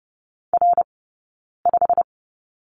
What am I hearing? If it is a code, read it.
Morse code, heard as a keyed tone, about 30 wpm, spaced as for 9 wpm, 690 Hz. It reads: R5
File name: course / 30:10wpm / 086 Lesson 15d - Number 5 - Two Characters 30wpm